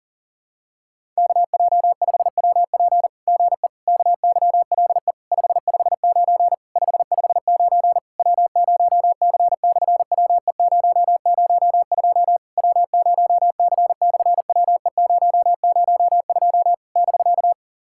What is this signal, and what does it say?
Morse code, heard as a keyed tone, about 40 wpm, 700 Hz. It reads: KJ5WP GE KYLE 559 559 W0C/WE002 W0C/WE002 <BK>
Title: SOTA QSO Practice 40wpm